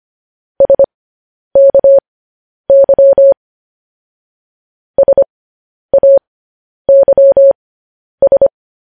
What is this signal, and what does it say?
Morse code, heard as a keyed tone, about 25 wpm, spaced as for 10 wpm, 550 Hz. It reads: SKY SAYS
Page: news-headlines-in-morse-code-at-10-wpm